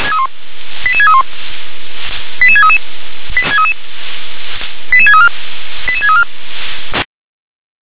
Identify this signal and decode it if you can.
Five-tone selective call (ZVEI), heard as quick runs of five tones, tones of about 70 ms. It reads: 88631 88633 88634 88634